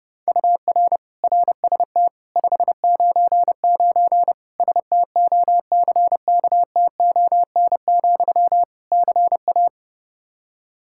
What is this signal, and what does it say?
Morse code, heard as a keyed tone, about 32 wpm, 700 Hz. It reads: UR RST 599 STOCKTON, CA
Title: K1USN SST QSO Practice 30wpm